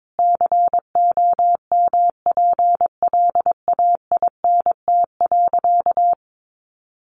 Morse code, heard as a keyed tone, about 22 wpm, 700 Hz. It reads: COMPLAINT.
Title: Sentences from Top 1400 Words 22wpm